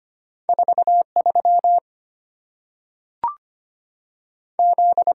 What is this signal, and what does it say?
Morse code, heard as a keyed tone, about 25 wpm, 700 Hz. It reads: 4 3 Z
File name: Two Letter-Number Combinations - 25wpm